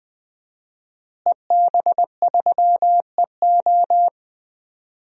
Morse code, heard as a keyed tone, about 20 wpm, 700 Hz. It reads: EB3EO
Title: DXCC-100 Call Signs - 5 Characters - Repeated 2x - 20wpm